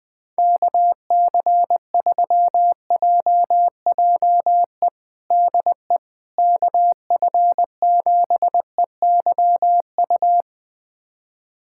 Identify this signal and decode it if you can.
Morse code, heard as a keyed tone, about 20 wpm, 700 Hz. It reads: KC3JJE DE KF7EYU
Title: USA Callsign DE Callsign 20wpm